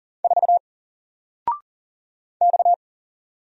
Morse code, heard as a keyed tone, about 40 wpm, 700 Hz. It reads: 4 X
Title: Single Letter-Number 40wpm